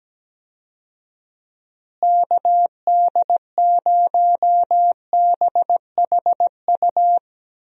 Morse code, heard as a keyed tone, about 17 wpm, 700 Hz. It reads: KD0BHU
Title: USA Call Signs - 6 Characters - Repeated 2x - 17wpm